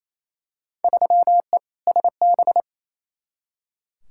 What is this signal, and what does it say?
Morse code, heard as a keyed tone, about 28 wpm, 700 Hz. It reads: 3E SB
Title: Field Day Exchanges Encore - 28wpm